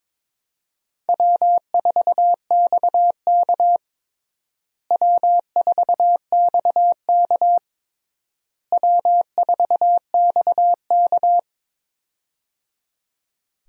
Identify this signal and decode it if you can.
Morse code, heard as a keyed tone, about 22 wpm, 700 Hz. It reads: W4XK W4XK W4XK
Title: USA Call Signs - 4 Characters - Repeated 3x - 22wpm